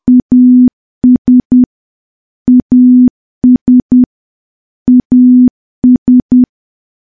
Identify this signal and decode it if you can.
Morse code, heard as a keyed tone, about 10 wpm, 250 Hz. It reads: AS AS AS
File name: morse / morse-training-10WPM